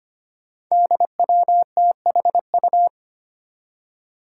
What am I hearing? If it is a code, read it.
Morse code, heard as a keyed tone, about 25 wpm, 700 Hz. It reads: DWTHU
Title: Five Challenging Letter-Number Combinations 25wpm